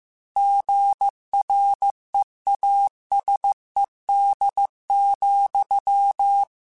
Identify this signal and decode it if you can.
Morse code, heard as a keyed tone, about 15 wpm, 800 Hz. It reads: GREASED,
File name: Chuckles / Chuckles-mp3s / Squeaky @13